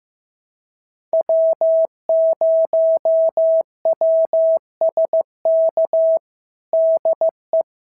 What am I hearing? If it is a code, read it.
Morse code, heard as a keyed tone, about 13 wpm, 650 Hz. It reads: W0WSK DE
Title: SOTA QSO Practice 15wpm